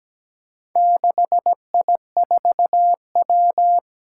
Morse code, heard as a keyed tone, about 17 wpm, 700 Hz. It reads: 6I4W
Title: Four Challenging Letter-Number Combinations 17wpm